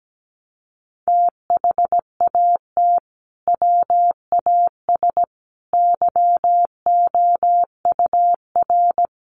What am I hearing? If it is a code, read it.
Morse code, heard as a keyed tone, about 17 wpm, 700 Hz. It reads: THAT WAS YOUR
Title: Sentences from Top 600 Words 17wpm